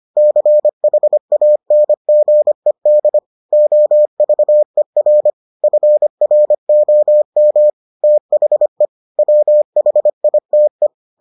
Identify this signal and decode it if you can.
Morse code, heard as a keyed tone, about 25 wpm, 600 Hz. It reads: CHANGED OVER FROM THE WHITE